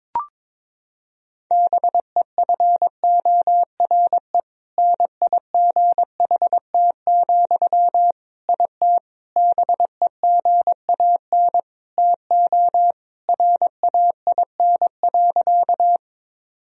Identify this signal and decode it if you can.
Morse code, heard as a keyed tone, about 22 wpm, 700 Hz. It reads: BEFORE NIGHT, IT BEGAN TO RAIN.
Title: Sentences from Top 1000 Words - Review Part 1 - 22wpm